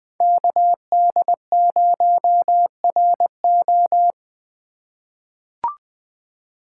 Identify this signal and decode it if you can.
Morse code, heard as a keyed tone, about 20 wpm, 700 Hz. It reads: KD0RO
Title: USA Call Signs - 5 Characters - 4x Character Spacing - 20wpm